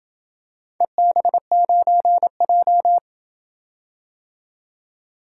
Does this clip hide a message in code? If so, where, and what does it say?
Yes: Morse code from 0.8 to 3.0 s, heard as a keyed tone, about 26 wpm, 700 Hz. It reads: EB9J